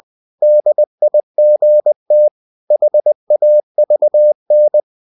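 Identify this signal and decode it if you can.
Morse code, heard as a keyed tone, about 20 wpm, 600 Hz. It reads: DIGT HAVN
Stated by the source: Morse code practice of short Danish words at 20 wpm, from CW med Gnister.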